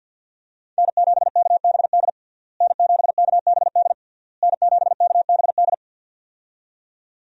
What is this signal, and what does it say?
Morse code, heard as a keyed tone, about 49 wpm, 700 Hz. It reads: N7KBD N7KBD N7KBD
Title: USA Call Signs - 5 Characters - Repeated 3x - 50wpm